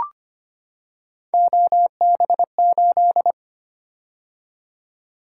Morse code, heard as a keyed tone, about 25 wpm, 700 Hz. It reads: OB8
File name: Three Letter-Number Combinations - 25wpm